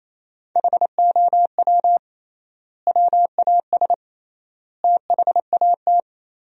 Morse code, heard as a keyed tone, about 28 wpm, 700 Hz. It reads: HOW WAS THAT
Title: 3x Word-Spacing - Sentences from Top 500 Words - 28wpm